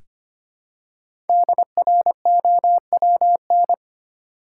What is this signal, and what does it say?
Morse code, heard as a keyed tone, about 25 wpm, 700 Hz. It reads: DROWN
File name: 5 Letter Words - First 3 Letters Given - 25wpm